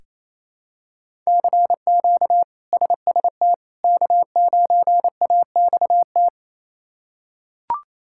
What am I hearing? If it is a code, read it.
Morse code, heard as a keyed tone, about 28 wpm, 700 Hz. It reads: CQ SST K9AXT
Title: K1USN SST QSO Practice - 28wpm